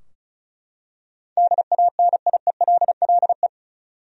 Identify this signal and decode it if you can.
Morse code, heard as a keyed tone, about 36 wpm, 700 Hz. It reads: DANIELLE
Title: USA Common Names - 35wpm